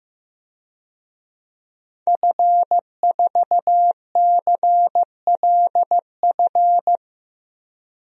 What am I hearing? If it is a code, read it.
Morse code, heard as a keyed tone, about 15 wpm, 700 Hz. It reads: F4CLF
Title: DXCC-100 Call Signs - 5 Characters - 15wpm